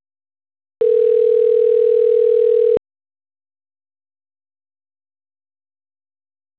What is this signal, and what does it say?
Ringback tone (North America), 440 + 480 Hz.